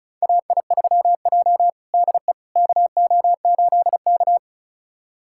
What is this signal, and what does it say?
Morse code, heard as a keyed tone, about 36 wpm, 700 Hz. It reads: AI3J DE KO8K
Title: USA Callsign DE Callsign - 4 Characters - 35wpm